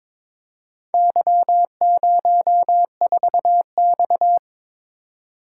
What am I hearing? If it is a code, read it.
Morse code, heard as a keyed tone, about 22 wpm, 690 Hz. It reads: Y04X
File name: practice / 22wpm / Four Letter-Number Combinations - 22wpm